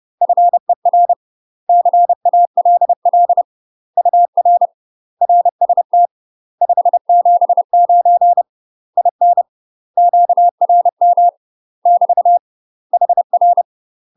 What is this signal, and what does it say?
Morse code, heard as a keyed tone, about 30 wpm, 700 Hz. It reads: FER CALL UR RST 579 IN QRM <BT> HR
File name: UK MASTER QSOs TRAINING / 30WPM / UK MASTER QSOs 0025